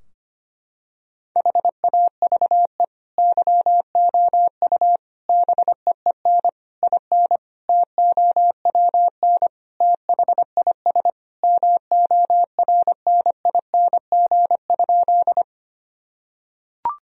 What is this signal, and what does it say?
Morse code, heard as a keyed tone, about 25 wpm, 700 Hz. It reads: HAVE YOU BEEN IN TOWN THIS MORNING?
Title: Sentences from Top 1000 Words - Review Part 2 - 25wpm